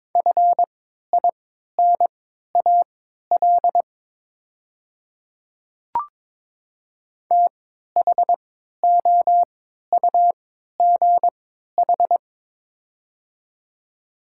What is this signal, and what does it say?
Morse code, heard as a keyed tone, about 22 wpm, spaced as for 12 wpm, 700 Hz. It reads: FINAL THOUGH
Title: 3x Character Spacing - Top 500 Words - 22wpm